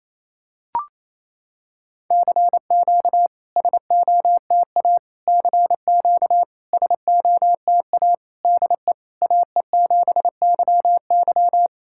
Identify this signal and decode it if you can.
Morse code, heard as a keyed tone, about 29 wpm, 690 Hz. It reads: CQ SOTA CQ SOTA DE AE7YY
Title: SOTA QSO Practice 28wpm